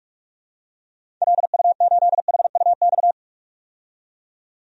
Morse code, heard as a keyed tone, about 42 wpm, 700 Hz. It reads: LU8HUX
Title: DXCC-100 Call Signs - 6 Characters - 45wpm